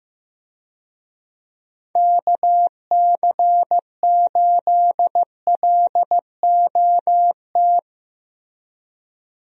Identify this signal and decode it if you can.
Morse code, heard as a keyed tone, about 15 wpm, 700 Hz. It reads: KC8LOT
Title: USA Call Signs - 6 Characters - Repeated 2x - 15wpm